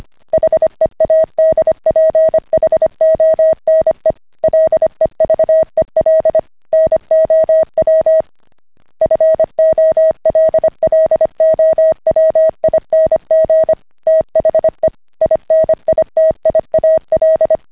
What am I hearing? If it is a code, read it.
Morse code, heard as a keyed tone, about 25 wpm, 640 Hz. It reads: HEADPHONE LEVEL NOW FOLLOWING THE INITIAL